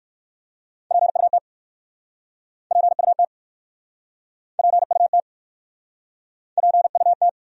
Morse code, heard as a keyed tone, about 64 wpm, 700 Hz. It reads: PUT PUT PUT PUT